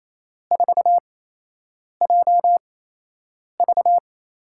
Morse code, heard as a keyed tone, about 28 wpm, spaced as for 8 wpm, 700 Hz. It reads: 4JV
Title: USA Call Signs - 6 Characters - 8x Character Spacing - 28wpm